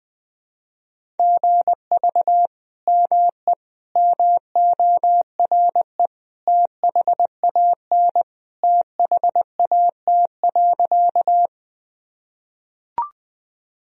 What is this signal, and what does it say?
Morse code, heard as a keyed tone, about 20 wpm, 700 Hz. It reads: GV ME MORE THAN THAT.
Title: Sentences from Top 200 Words and Abbreviations 20wpm